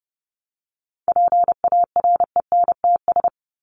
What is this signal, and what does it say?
Morse code, heard as a keyed tone, about 30 wpm, 700 Hz. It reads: PARENTS
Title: ES:S Suffix - Top 100 Words with Base Words 30wpm